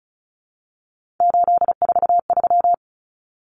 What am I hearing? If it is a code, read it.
Morse code, heard as a keyed tone, about 34 wpm, 700 Hz. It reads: 843
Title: Common Cut Numbers - 3-Digit 35wpm